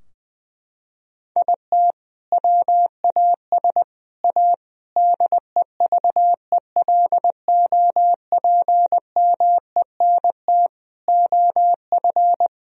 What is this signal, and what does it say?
Morse code, heard as a keyed tone, about 21 wpm, 700 Hz. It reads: IT WAS A DEVELOPMENT OF
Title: Sentences from Top 1000 Words - Review Part 4 - 20wpm